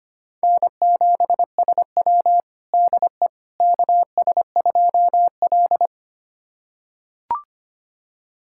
Morse code, heard as a keyed tone, about 25 wpm, 700 Hz. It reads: N7SW DE KS2L
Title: USA Callsign DE Callsign - 4 Characters - 25wpm